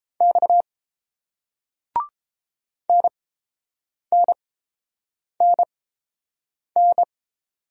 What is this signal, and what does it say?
Morse code, heard as a keyed tone, about 31 wpm, spaced as for 7 wpm, 690 Hz. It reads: X NNNN